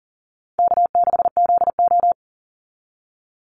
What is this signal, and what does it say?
Morse code, heard as a keyed tone, about 40 wpm, 700 Hz. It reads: K6ZO
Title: Easy Call Signs Repeated 2x - 40wpm